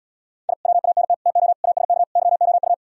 Morse code, heard as a keyed tone, about 77 wpm, 700 Hz. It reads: I WANT TO ASK YOU